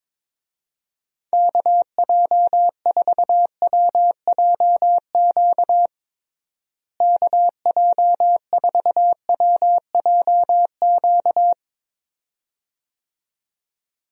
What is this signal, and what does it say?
Morse code, heard as a keyed tone, about 22 wpm, 700 Hz. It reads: KJ4WJQ KJ4WJQ
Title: USA Call Signs - 6 Characters - Repeated 2x - 22wpm